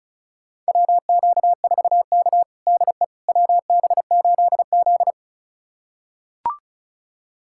Morse code, heard as a keyed tone, about 36 wpm, 700 Hz. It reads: WQ4K DE WB8Z
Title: USA Callsign DE Callsign - 4 Characters - 35wpm